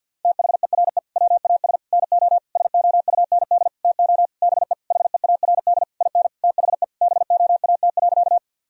Morse code, heard as a keyed tone, about 51 wpm, 700 Hz. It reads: THERE WAS NO SOUND TO BE HEARD IN THE BOAT.
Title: Sentences from Top 1000 Words - Review Part 3 - 50wpm